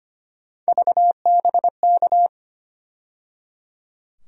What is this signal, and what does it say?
Morse code, heard as a keyed tone, about 25 wpm, 700 Hz. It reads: VBK
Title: USA Call Sign Postfixes - 3 Letters 25wpm